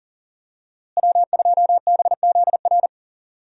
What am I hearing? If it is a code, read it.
Morse code, heard as a keyed tone, about 40 wpm, 700 Hz. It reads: W2BZR
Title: USA Call Signs - 5 Characters - Repeated 3x - 40wpm